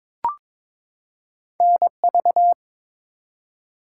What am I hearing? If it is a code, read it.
Morse code, heard as a keyed tone, about 22 wpm, 700 Hz. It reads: NV
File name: ARRL Sections v2 - 22wpm